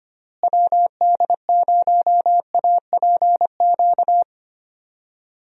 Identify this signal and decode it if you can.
Morse code, heard as a keyed tone, about 25 wpm, 700 Hz. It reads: WD0APQ
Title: USA Call Signs - 6 Characters - Repeated 3x - 25wpm